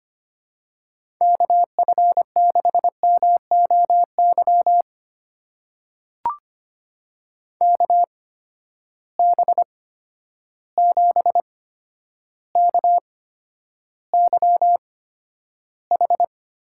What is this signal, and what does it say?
Morse code, heard as a keyed tone, about 25 wpm, spaced as for 11 wpm, 700 Hz. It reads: KF6MOY KB7KYH
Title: USA Call Signs - 6 Characters - 8x Character Spacing - 25wpm